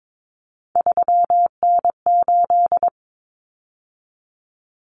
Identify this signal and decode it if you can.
Morse code, heard as a keyed tone, about 22 wpm, 700 Hz. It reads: 3N8
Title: Common Cut Numbers - 3-Digit 22wpm